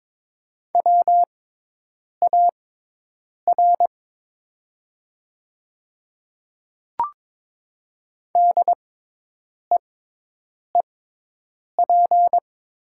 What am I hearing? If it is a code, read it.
Morse code, heard as a keyed tone, about 22 wpm, spaced as for 8 wpm, 690 Hz. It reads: WAR DEEP
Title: 6x Character Spacing - Top 500 Words - 22wpm